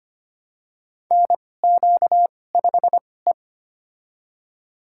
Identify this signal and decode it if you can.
Morse code, heard as a keyed tone, about 25 wpm, spaced as for 18 wpm, 700 Hz. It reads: NQ5E
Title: USA Call Signs - 2x Character Spacing - 25wpm